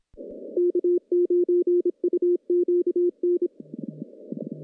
Morse code, heard as a keyed tone, about 26 wpm, 350 Hz. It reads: K9UQN
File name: contests / CW-SS-k5tr- -1300